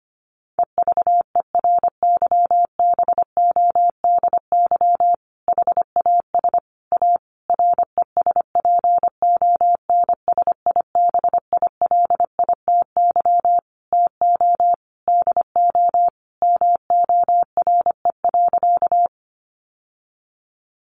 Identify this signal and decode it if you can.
Morse code, heard as a keyed tone, about 25 wpm, 700 Hz. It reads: EVERYBODY HAS A RESPONSIBILITY TO DO MORE.
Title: Sentences from Top 1400 Words 25wpm